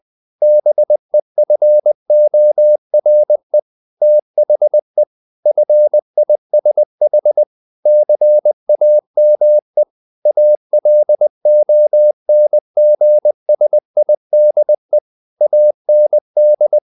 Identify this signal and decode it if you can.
Morse code, heard as a keyed tone, about 20 wpm, 600 Hz. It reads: BEFORE THE FISH CAME ALONGSIDE AND
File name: the old man and the sea part 2 20wpm